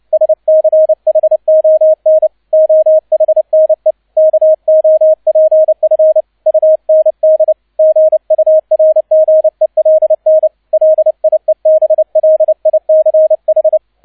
Morse code, heard as a keyed tone, about 29 wpm, 630 Hz, with radio noise. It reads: SCHON OHNE KOPF UND GURGELN LIEBLICH
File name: 002 Zweiter Streich 30wpm